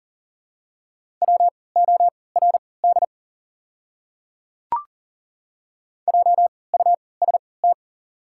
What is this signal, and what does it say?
Morse code, heard as a keyed tone, about 40 wpm, spaced as for 23 wpm, 700 Hz. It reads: WORD JUST
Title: 3x Character Spacing - Top 200 Words - 40wpm